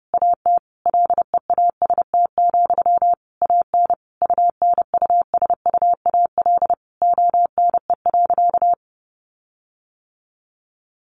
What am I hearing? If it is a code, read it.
Morse code, heard as a keyed tone, about 30 wpm, 700 Hz. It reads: AT LEAST, AN UNUSUAL ONE.
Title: Sentences from Top 1400 Words 30wpm